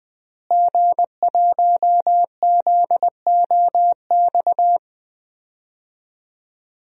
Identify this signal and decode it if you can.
Morse code, heard as a keyed tone, about 20 wpm, 700 Hz. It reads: G1ZOX
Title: DXCC-100 Call Signs - 5 Characters - Repeated 3x - 20wpm